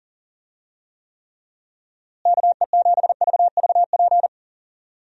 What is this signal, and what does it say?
Morse code, heard as a keyed tone, about 40 wpm, 700 Hz. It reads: KE7VVP
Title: USA Call Signs - 6 Characters - Repeated 3x - 40wpm